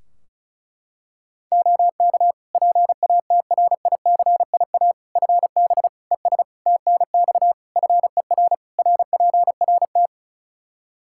Morse code, heard as a keyed tone, about 36 wpm, 700 Hz. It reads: OK PATRICIA FB ES TNX FER RPRT
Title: Practice QSO - Exchange 2 - Info - 35wpm